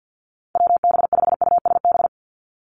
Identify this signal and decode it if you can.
Morse code, heard as a keyed tone, about 50 wpm, 700 Hz. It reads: WB5USB
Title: NA SOTA Activator Callsigns 50wpm